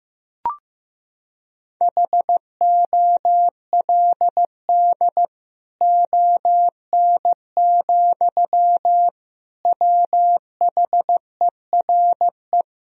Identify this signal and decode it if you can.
Morse code, heard as a keyed tone, about 15 wpm, 700 Hz. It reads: HOLD ON, WHERE
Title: Sentences from Top 1000 Words - Review Part 2 - 15wpm